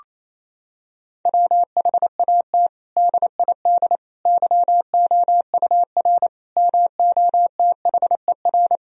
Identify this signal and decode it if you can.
Morse code, heard as a keyed tone, about 28 wpm, 700 Hz. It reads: WHAT DID YOUR MOTHER
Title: Sentences from Top 1000 Words - Review Part 3 - 28wpm